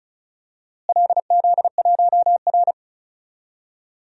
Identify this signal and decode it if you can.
Morse code, heard as a keyed tone, about 34 wpm, 700 Hz. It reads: LZ1R
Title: DXCC-100 Call Signs - 4 Characters - Repeated 2x - 35wpm